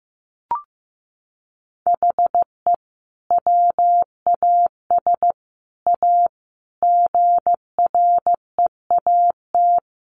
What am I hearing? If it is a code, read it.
Morse code, heard as a keyed tone, about 15 wpm, 700 Hz. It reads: HE WAS A GREAT
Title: Sentences from Top 1200 Words 15wpm